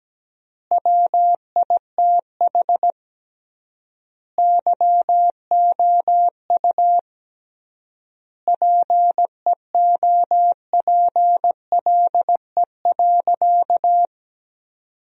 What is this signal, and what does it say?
Morse code, heard as a keyed tone, about 17 wpm, 700 Hz. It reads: WITH YOU PEOPLE.
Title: 3x Word-Spacing - Sentences from Top 300 Words - 17wpm